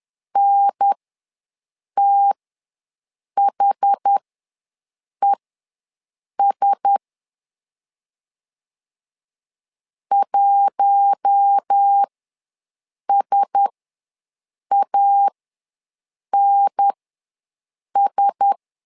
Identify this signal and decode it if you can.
Morse code, heard as a keyed tone, about 10 wpm, spaced as for 6 wpm, 800 Hz. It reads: NTHES 1SANS
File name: LEZIONI / 4 Q1S ws 10wpm 2208